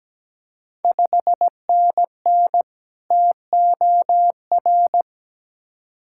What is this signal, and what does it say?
Morse code, heard as a keyed tone, about 17 wpm, 690 Hz. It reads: 5NN TOR
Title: Ontario QSO Party Exchange 17wpm